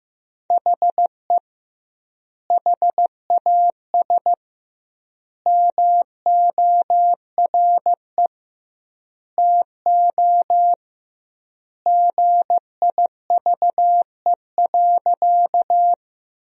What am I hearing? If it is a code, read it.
Morse code, heard as a keyed tone, about 15 wpm, 700 Hz. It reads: HE HAS MORE TO GIVE.
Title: 2x Word-Spacing - Sentences from Top 200 Words - 15wpm